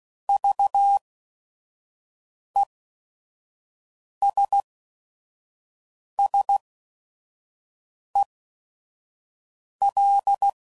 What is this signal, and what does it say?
Morse code, heard as a keyed tone, about 16 wpm, spaced as for 5 wpm, 800 Hz. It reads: VESSEL
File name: Prefixes / By First Letter / Vs